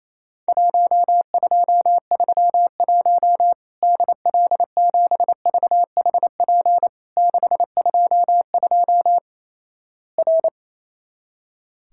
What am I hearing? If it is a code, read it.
Morse code, heard as a keyed tone, about 28 wpm, 700 Hz. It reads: 1231 DL7VHP 622 R
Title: Worked All Europe DX Contest - QTC Exchange - 28wpm